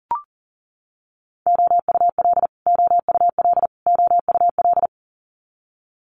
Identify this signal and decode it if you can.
Morse code, heard as a keyed tone, about 37 wpm, 700 Hz. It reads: OUL OUL OUL
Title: Multi-letter Phonemes Repeated 3x 40wpm